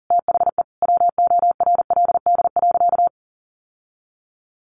Morse code, heard as a keyed tone, about 40 wpm, 700 Hz. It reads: THE WORLD.
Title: Sentences from Top 1200 Words 40wpm